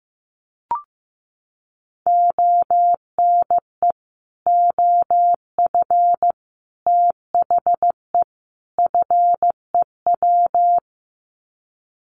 Morse code, heard as a keyed tone, about 15 wpm, 700 Hz. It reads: ONE OF THE FEW
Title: Sets of 4 Words 15wpm